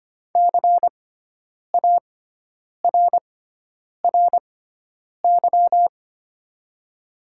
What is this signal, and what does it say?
Morse code, heard as a keyed tone, about 25 wpm, spaced as for 9 wpm, 700 Hz. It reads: CARRY
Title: 6x Character Spacing - Top 300 Words - 25wpm